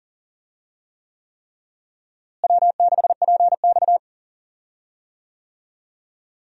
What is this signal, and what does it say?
Morse code, heard as a keyed tone, about 40 wpm, 700 Hz. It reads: W6PX
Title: USA Call Signs - 4 Characters - Repeated 3x - 40wpm